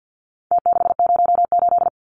Morse code, heard as a keyed tone, about 48 wpm, 700 Hz. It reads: T608
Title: Numbers - Contest Style - 4-Digit 50wpm